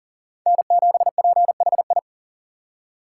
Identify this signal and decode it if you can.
Morse code, heard as a keyed tone, about 40 wpm, 700 Hz. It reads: N7PHI
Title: K1USN SST QSO Practice - 40wpm